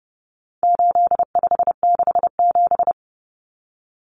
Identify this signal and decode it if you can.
Morse code, heard as a keyed tone, about 29 wpm, 700 Hz. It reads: 8567
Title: Numbers - 4-Digit 30wpm